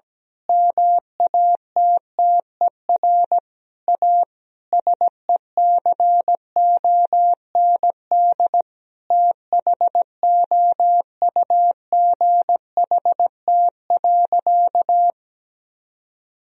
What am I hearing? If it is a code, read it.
Morse code, heard as a keyed tone, about 17 wpm, 700 Hz. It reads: MATTER A SECOND THOUGHT.
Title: Sentences from Top 1000 Words - Review Part 2 - 17wpm